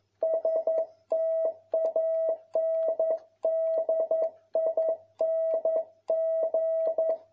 Morse code, heard as a keyed tone, about 11 wpm, 650 Hz. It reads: STANDING
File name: jh8cbh / 2021cw